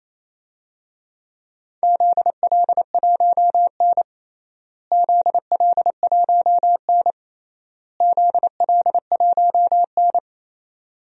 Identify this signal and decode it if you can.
Morse code, heard as a keyed tone, about 28 wpm, 700 Hz. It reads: ZL1N ZL1N ZL1N
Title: DXCC-100 Call Signs - 4 Characters - Repeated 3x - 28wpm